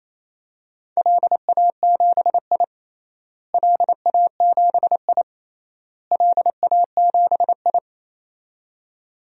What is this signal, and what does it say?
Morse code, heard as a keyed tone, about 28 wpm, 700 Hz. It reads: LA7I LA7I LA7I
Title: DXCC-100 Call Signs - 4 Characters - Repeated 3x - 28wpm